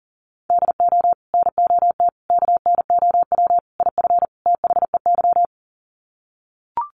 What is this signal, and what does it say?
Morse code, heard as a keyed tone, about 40 wpm, 700 Hz. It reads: DO NOT KNOW IF THEY
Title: Sets of 5 Words Encore 40wpm